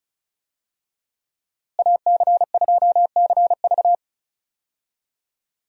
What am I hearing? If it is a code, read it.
Morse code, heard as a keyed tone, about 36 wpm, 700 Hz. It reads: AC2CV
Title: USA Call Signs - 5 Characters - Repeated 3x - 35wpm